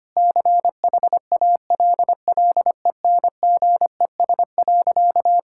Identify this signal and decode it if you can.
Morse code, heard as a keyed tone, about 25 wpm, 700 Hz. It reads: CHALLENGES.
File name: practice / 25wpm / Difficult Sentences with Hints - 25wpm